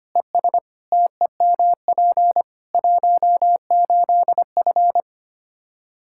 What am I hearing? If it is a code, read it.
Morse code, heard as a keyed tone, about 25 wpm, 700 Hz. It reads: ES TEMP 18F
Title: Practice QSO - Exchange 2 - Info - 25wpm